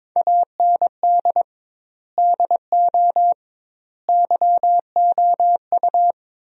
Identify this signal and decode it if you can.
Morse code, heard as a keyed tone, about 22 wpm, 700 Hz. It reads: AND DO YOU
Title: 2x Word-Spacing - Sentences from Top 100 Words - 22wpm